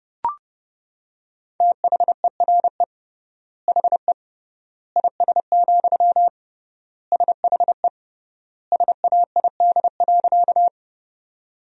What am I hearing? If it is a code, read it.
Morse code, heard as a keyed tone, about 30 wpm, 700 Hz. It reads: THERE HE IS, SHE SAID.
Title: 3x Word-Spacing - Sentences from Top 100 Words - 30wpm